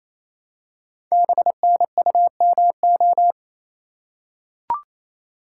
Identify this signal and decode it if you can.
Morse code, heard as a keyed tone, about 28 wpm, 700 Hz. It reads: BNUMO
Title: Code Groups - 5 Letters - 28wpm